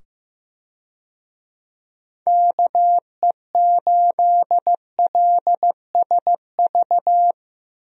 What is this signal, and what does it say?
Morse code, heard as a keyed tone, about 15 wpm, 700 Hz. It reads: KE8LSV
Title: USA Call Signs - 6 Characters - 15wpm